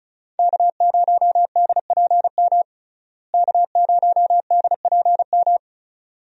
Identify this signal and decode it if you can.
Morse code, heard as a keyed tone, about 36 wpm, 700 Hz. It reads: K0DPM K0DPM
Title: USA Call Signs - 5 Characters - Repeated 2x - 35wpm